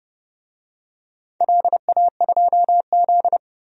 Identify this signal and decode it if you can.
Morse code, heard as a keyed tone, about 30 wpm, 700 Hz. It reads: LA2Z
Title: DXCC-100 Call Signs - 4 Characters - 30wpm